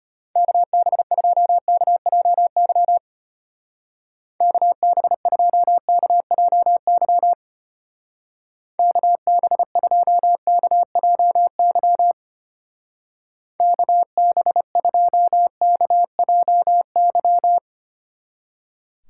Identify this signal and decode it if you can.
Morse code, heard as a keyed tone, about 32 wpm, 700 Hz. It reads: KB2KJY KB2KJY KB2KJY KB2KJY